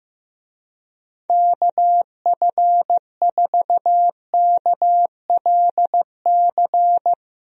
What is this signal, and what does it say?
Morse code, heard as a keyed tone, about 15 wpm, 700 Hz. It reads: KF4KLC
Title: USA Call Signs - 6 Characters - 15wpm